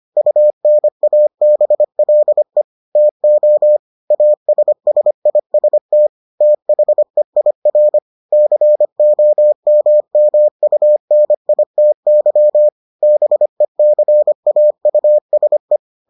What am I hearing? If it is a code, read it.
Morse code, heard as a keyed tone, about 25 wpm, 600 Hz. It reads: UNABLE TO ASSIST THEIR COMMUNITY BECAUSE